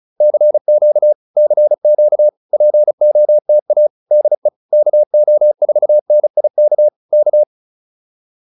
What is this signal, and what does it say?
Morse code, heard as a keyed tone, about 34 wpm, 600 Hz. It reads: CQ CQ POTA DE KO4NIK K
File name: POTA302 35